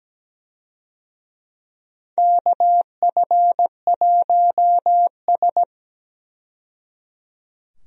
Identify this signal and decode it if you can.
Morse code, heard as a keyed tone, about 17 wpm, 700 Hz. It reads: KF1S
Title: USA Call Signs - 4 Characters - 17wpm